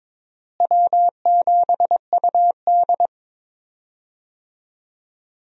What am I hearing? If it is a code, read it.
Morse code, heard as a keyed tone, about 22 wpm, 700 Hz. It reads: W7UD
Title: USA Call Signs - 4 Characters - Repeated 2x - 22wpm